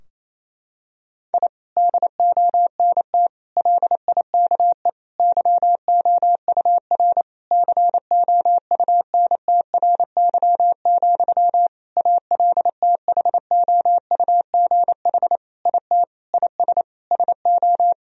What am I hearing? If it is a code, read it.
Morse code, heard as a keyed tone, about 27 wpm, 700 Hz. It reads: I DONT LIKE YOUR COUNTRY, ALTHOUGH IT IS SO
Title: Sentences from Top 1000 Words - Review Part 3 - 28wpm